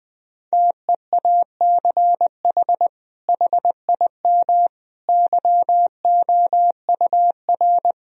Morse code, heard as a keyed tone, about 20 wpm, 700 Hz. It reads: TEACH HIM YOUR
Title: Sentences from Top 1000 Words - Review Part 3 - 20wpm